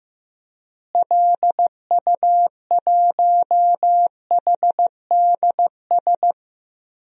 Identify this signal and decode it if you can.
Morse code, heard as a keyed tone, about 15 wpm, 700 Hz. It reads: LU1HDS
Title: DXCC-100 Call Signs - 6 Characters - Repeated 3x - 15wpm